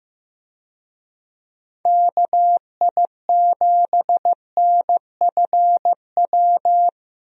Morse code, heard as a keyed tone, about 15 wpm, 700 Hz. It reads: KI7NFW
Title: USA Call Signs - 6 Characters - Repeated 2x - 15wpm